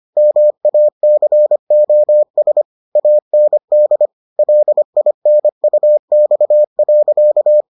Morse code, heard as a keyed tone, about 25 wpm, 600 Hz. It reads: MACOS AND LINUX.